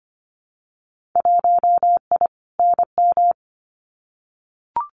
Morse code, heard as a keyed tone, about 25 wpm, 700 Hz. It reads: 1I NM
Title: Winter Field Day Exchanges Encore 25wpm